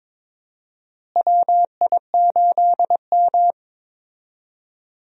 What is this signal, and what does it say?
Morse code, heard as a keyed tone, about 21 wpm, 700 Hz. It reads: WI8M
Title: USA Call Signs - 4 Characters - Repeated 2x - 22wpm